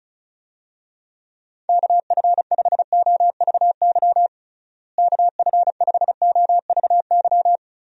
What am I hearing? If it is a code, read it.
Morse code, heard as a keyed tone, about 34 wpm, 700 Hz. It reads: KF5OVY KF5OVY
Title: USA Call Signs - 6 Characters - Repeated 2x - 35wpm